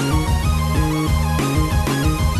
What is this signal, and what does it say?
There is no code.